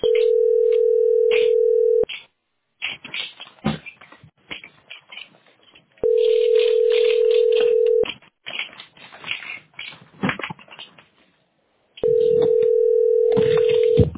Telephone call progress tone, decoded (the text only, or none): ringback tone